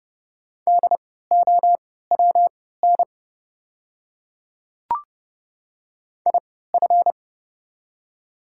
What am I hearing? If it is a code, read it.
Morse code, heard as a keyed tone, about 30 wpm, spaced as for 17 wpm, 690 Hz. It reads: DOWN IF
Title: 3x Character Spacing - Top 100 Words - 30wpm